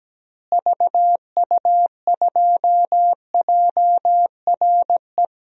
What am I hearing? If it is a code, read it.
Morse code, heard as a keyed tone, about 17 wpm, 700 Hz. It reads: VU2JRE
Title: DXCC-100 Call Signs - 6 Characters - Repeated 3x - 17wpm